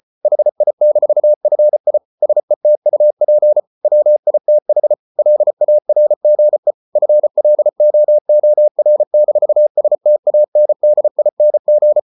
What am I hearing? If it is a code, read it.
Morse code, heard as a keyed tone, about 34 wpm, 600 Hz. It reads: HI-FI SETUP WITH LARGE FLOOR-STANDING